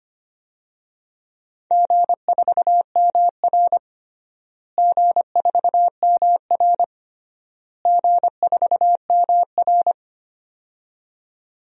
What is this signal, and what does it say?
Morse code, heard as a keyed tone, about 25 wpm, 700 Hz. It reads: G4MR G4MR G4MR
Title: DXCC-100 Call Signs - 4 Characters - Repeated 3x - 25wpm